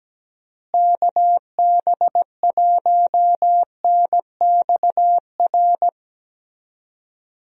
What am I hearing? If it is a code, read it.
Morse code, heard as a keyed tone, about 17 wpm, 700 Hz. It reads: KB1NXR
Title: USA Call Signs - 6 Characters - Repeated 3x - 17wpm